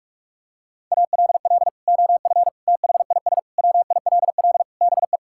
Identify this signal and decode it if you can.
Morse code, heard as a keyed tone, about 46 wpm, 700 Hz. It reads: ALL OF THIS WILL BE
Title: Sentences from Top 1000 Words - Review Part 6 - 45wpm